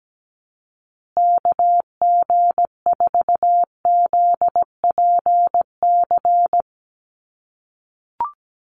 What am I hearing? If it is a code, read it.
Morse code, heard as a keyed tone, about 17 wpm, 700 Hz. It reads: KG4ZPC
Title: USA Call Signs Repeated 2x 17wpm